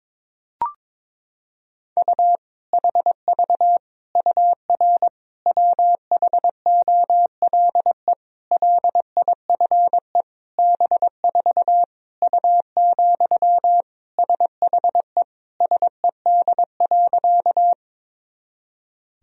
Morse code, heard as a keyed tone, about 22 wpm, 700 Hz. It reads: U HV UR WHOLE LIFE B4 U, SHE SED.
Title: Sentences from Top 400 Words and Abbreviations 22wpm